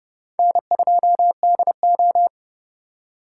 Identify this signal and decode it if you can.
Morse code, heard as a keyed tone, about 28 wpm, 700 Hz. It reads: N2DO